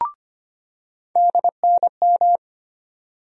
Morse code, heard as a keyed tone, about 25 wpm, 700 Hz. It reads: DNM
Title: USA Call Sign Postfixes - 3 Letters 25wpm